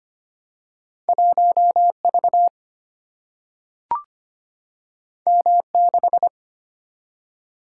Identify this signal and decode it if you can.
Morse code, heard as a keyed tone, about 25 wpm, 700 Hz. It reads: 1 V M 6
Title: Single-Letter Single-Number 25wpm